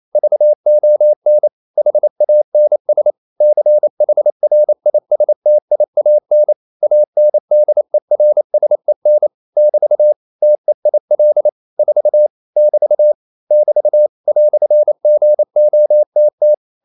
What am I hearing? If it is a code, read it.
Morse code, heard as a keyed tone, about 28 wpm, 600 Hz. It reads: VON HANS CHRISTIAN ANDERSEN <BT> TEIL 4 <BT> <BT> "GOTT
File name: Liebespaar4 28wpm